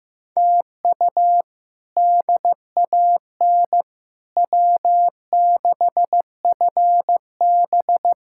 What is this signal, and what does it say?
Morse code, heard as a keyed tone, about 15 wpm, 700 Hz. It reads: TU DAN W6FB
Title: K1USN SST QSO Practice - 15wpm